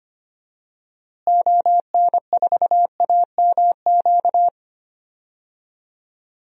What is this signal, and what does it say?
Morse code, heard as a keyed tone, about 25 wpm, 700 Hz. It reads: ON4AMQ
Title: DXCC-100 Call Signs - 6 Characters - Repeated 2x - 25wpm